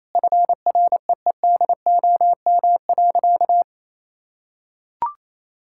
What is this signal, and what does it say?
Morse code, heard as a keyed tone, about 28 wpm, 700 Hz. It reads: FREEDOM.
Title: Sentences from Top 1000 Words - Review Part 5 - 28wpm